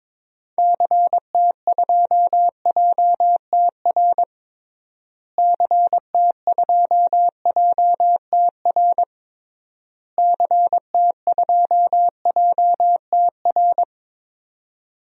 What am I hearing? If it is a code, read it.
Morse code, heard as a keyed tone, about 22 wpm, 700 Hz. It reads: CT2JTR CT2JTR CT2JTR
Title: DXCC-100 Call Signs - 6 Characters - Repeated 3x - 22wpm